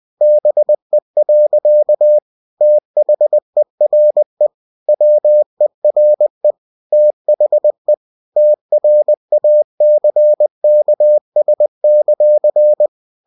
Morse code, heard as a keyed tone, about 20 wpm, 600 Hz. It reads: BE. THERE WERE THE TRACKS;
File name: cw-books / winnie-the-pooh / winnie-20wpm-page-0024